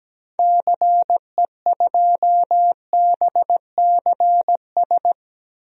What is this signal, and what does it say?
Morse code, heard as a keyed tone, about 17 wpm, 700 Hz. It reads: CE2BCS
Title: DXCC-100 Call Signs - 6 Characters - Repeated 2x - 17wpm